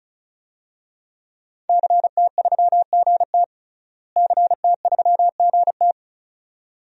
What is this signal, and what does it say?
Morse code, heard as a keyed tone, about 34 wpm, 700 Hz. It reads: CT3GT CT3GT
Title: DXCC-100 Call Signs - 5 Characters - Repeated 2x - 35wpm